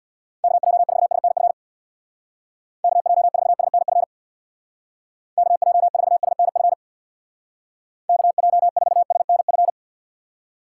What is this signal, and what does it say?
Morse code, heard as a keyed tone, about 59 wpm, 700 Hz. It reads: KJ4SNF KJ4SNF KJ4SNF KJ4SNF
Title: Speed Racing - USA 2x3 Call Signs 50wpm